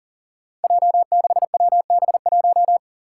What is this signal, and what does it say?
Morse code, heard as a keyed tone, about 42 wpm, 700 Hz. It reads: J6WB1
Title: Five Challenging Letter-Number Combinations 40wpm